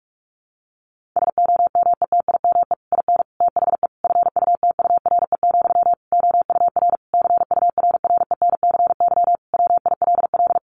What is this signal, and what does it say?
Morse code, heard as a keyed tone, about 44 wpm, 700 Hz. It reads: SOMETIME IN THE FUTURE, OUR CURRENCY WILL
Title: Sentences from Top 1500 Words 45wpm